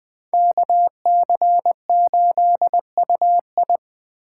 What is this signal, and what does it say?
Morse code, heard as a keyed tone, about 20 wpm, 700 Hz. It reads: KC8UI